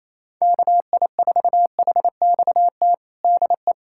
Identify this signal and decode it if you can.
Morse code, heard as a keyed tone, about 28 wpm, 700 Hz. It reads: KI4HXT DE
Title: USA Callsign DE Callsign - 6 Characters - 28wpm